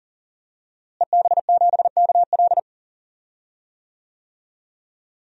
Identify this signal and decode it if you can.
Morse code, heard as a keyed tone, about 37 wpm, 700 Hz. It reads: EB7KL
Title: DXCC-100 Call Signs - 5 Characters - Repeated 2x - 40wpm